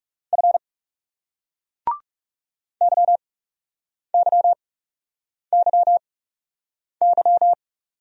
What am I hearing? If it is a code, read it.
Morse code, heard as a keyed tone, about 38 wpm, spaced as for 8 wpm, 700 Hz. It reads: F YYYY